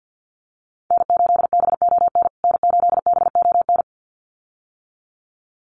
Morse code, heard as a keyed tone, about 50 wpm, 700 Hz. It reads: N8BON N8BON
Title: USA Call Signs Repeated 2x 50wpm